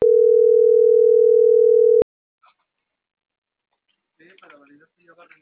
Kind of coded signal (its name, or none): telephone call progress tone